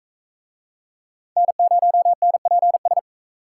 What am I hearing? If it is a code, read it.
Morse code, heard as a keyed tone, about 43 wpm, 700 Hz. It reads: N0NPS